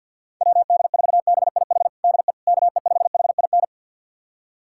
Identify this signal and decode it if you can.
Morse code, heard as a keyed tone, about 52 wpm, 700 Hz. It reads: WD4BIH DE KE5HIN
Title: USA Callsign DE Callsign 50wpm